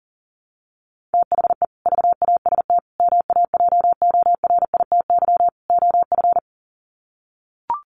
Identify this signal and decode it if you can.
Morse code, heard as a keyed tone, about 40 wpm, 700 Hz. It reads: THE VAST MAJORITY OF
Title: Sets of 4 Words 40wpm